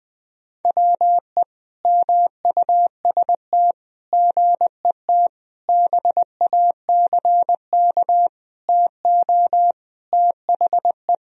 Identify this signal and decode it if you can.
Morse code, heard as a keyed tone, about 20 wpm, 700 Hz. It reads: WE MUST GET BACK TO THE
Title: Sentences from Top 1000 Words - Review Part 2 - 20wpm